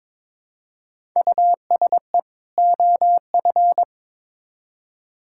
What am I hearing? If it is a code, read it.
Morse code, heard as a keyed tone, about 21 wpm, 700 Hz. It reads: USE OF
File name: Build A Phrase - 2-Word Phrases - 22wpm